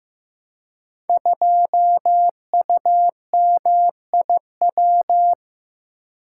Morse code, heard as a keyed tone, about 15 wpm, 700 Hz. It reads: 2UMIW
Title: Code Groups - 5 Letter:Numbers - 15wpm